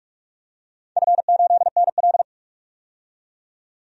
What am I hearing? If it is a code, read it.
Morse code, heard as a keyed tone, about 46 wpm, 700 Hz. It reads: F8NL
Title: DXCC-100 Call Signs - 4 Characters - 45wpm